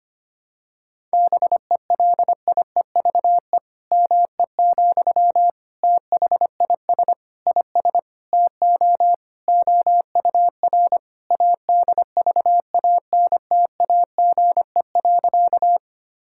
Morse code, heard as a keyed tone, about 25 wpm, 700 Hz. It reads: BELIEVE ME, THIS IS TO OUR ADVANTAGE.
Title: Sentences from Top 1000 Words - Review Part 6 - 25wpm